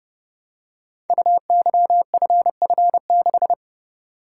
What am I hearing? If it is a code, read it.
Morse code, heard as a keyed tone, about 29 wpm, 700 Hz. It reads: UYFF6
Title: Code Groups - 5 Letter:Numbers - 30wpm